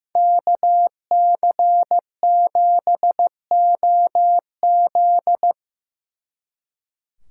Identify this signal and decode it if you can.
Morse code, heard as a keyed tone, about 17 wpm, 700 Hz. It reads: KC7OZ
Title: Speed Racing - USA 2x2 Call Signs 15wpm